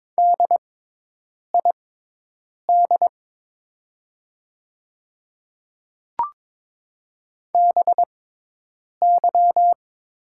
Morse code, heard as a keyed tone, about 22 wpm, spaced as for 8 wpm, 700 Hz. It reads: DID BY
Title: 6x Character Spacing - Top 200 Words - 22wpm